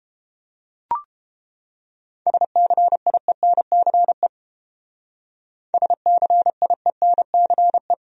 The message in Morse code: SCIENCE SCIENCE

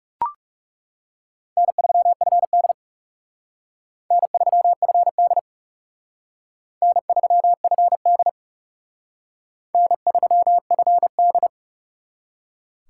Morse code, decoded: N3FD N3FD N3FD N3FD

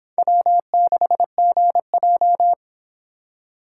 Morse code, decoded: W6GJ

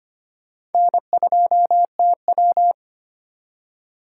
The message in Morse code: N2TW

